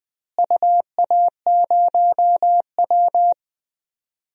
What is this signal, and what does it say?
Morse code, heard as a keyed tone, about 20 wpm, 700 Hz. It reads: UA0W
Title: DXCC-100 Call Signs - 4 Characters - Repeated 3x - 20wpm